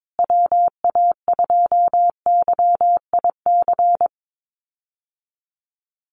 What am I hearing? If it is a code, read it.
Morse code, heard as a keyed tone, about 22 wpm, 700 Hz. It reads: WA2YIC
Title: USA Call Signs Repeated 3x 22wpm